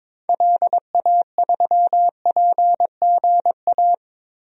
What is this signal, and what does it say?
Morse code, heard as a keyed tone, about 22 wpm, 700 Hz. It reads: LA3PGA
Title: DXCC-100 Call Signs - 6 Characters - Repeated 3x - 22wpm